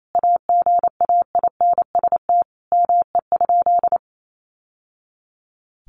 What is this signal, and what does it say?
Morse code, heard as a keyed tone, about 28 wpm, 700 Hz. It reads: AGAINST ME?
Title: Sentences from Top 1000 Words 28wpm